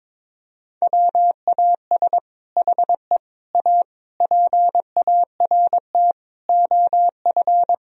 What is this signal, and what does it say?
Morse code, heard as a keyed tone, about 22 wpm, 700 Hz. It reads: WAS HE A PART OF
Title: Sentences from Top 1000 Words - Review Part 4 - 22wpm